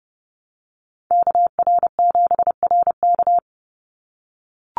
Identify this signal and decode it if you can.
Morse code, heard as a keyed tone, about 30 wpm, 700 Hz. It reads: KR7RK
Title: NA SOTA Activator Callsigns 30wpm